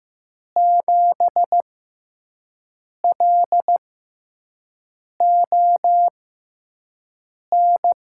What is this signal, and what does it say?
Morse code, heard as a keyed tone, about 15 wpm, spaced as for 5 wpm, 700 Hz. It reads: 7LON
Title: USA Call Signs - 6 Characters - 6x Character Spacing - 15wpm